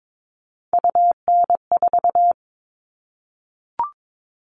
Morse code, heard as a keyed tone, about 22 wpm, 700 Hz. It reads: UN4
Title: All Cut Numbers - 3-Digit 22wpm